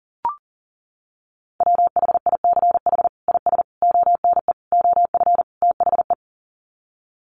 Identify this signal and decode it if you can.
Morse code, heard as a keyed tone, about 40 wpm, 700 Hz. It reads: WHICH IS ONE OF THE
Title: Sets of 5 Words Encore 40wpm